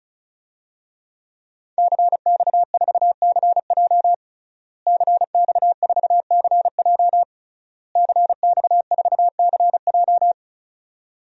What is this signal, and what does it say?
Morse code, heard as a keyed tone, about 36 wpm, 700 Hz. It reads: CX4CJ CX4CJ CX4CJ
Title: DXCC-100 Call Signs - 5 Characters - Repeated 3x - 35wpm